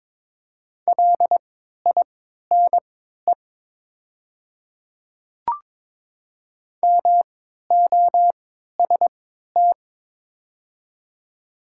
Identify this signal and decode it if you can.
Morse code, heard as a keyed tone, about 21 wpm, spaced as for 12 wpm, 700 Hz. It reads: LINE MOST
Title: 3x Character Spacing - Top 200 Words - 22wpm